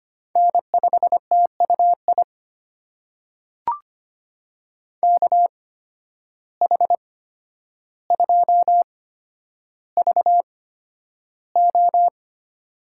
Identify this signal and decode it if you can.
Morse code, heard as a keyed tone, about 25 wpm, spaced as for 11 wpm, 700 Hz. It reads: N5TUI KH2VO
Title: USA Call Signs - 5 Characters - 8x Character Spacing - 25wpm